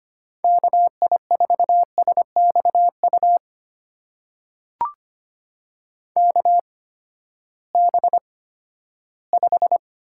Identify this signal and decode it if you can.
Morse code, heard as a keyed tone, about 25 wpm, spaced as for 14 wpm, 700 Hz. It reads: KI4SXU KB5
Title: USA Call Signs - 6 Characters - 8x Character Spacing - 25wpm